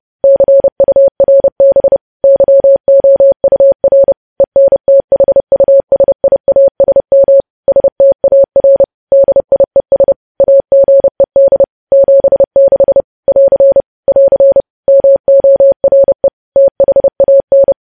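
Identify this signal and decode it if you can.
Morse code, heard as a keyed tone, about 30 wpm, 550 Hz. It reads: CURB YOUR ENTHUSIASM STAR DIES AGED 76 <AR> <AR> MORE THAN
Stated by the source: Morse news headlines sent at 30 wpm.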